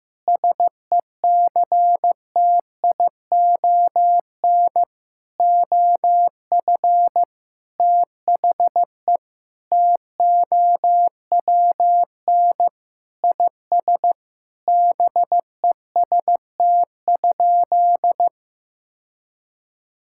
Morse code, heard as a keyed tone, about 15 wpm, 700 Hz. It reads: SECTION OF THE TOWN IS BEST?
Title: Sentences from Top 1000 Words - Review Part 5 - 15wpm